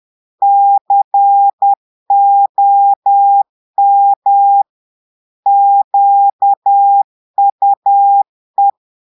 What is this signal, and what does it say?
Morse code, heard as a keyed tone, about 10 wpm, 800 Hz. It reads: COM QUE
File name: portugais texte en clair 2 vitesse 10 mots minutes